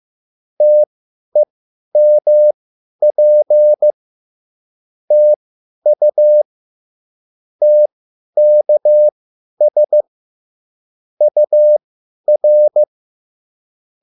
Morse code, heard as a keyed tone, about 15 wpm, spaced as for 10 wpm, 600 Hz. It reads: TEMP TU TKS UR
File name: QSL / CW lesson